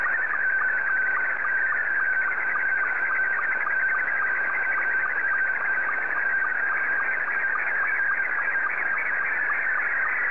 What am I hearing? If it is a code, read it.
SSTV (slow-scan television), Scottie 2.